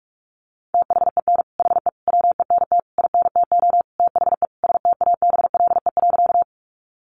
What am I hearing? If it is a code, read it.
Morse code, heard as a keyed tone, about 44 wpm, 700 Hz. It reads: THEN HE WENT INTO THE STABLE.
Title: Sentences from Top 1500 Words 45wpm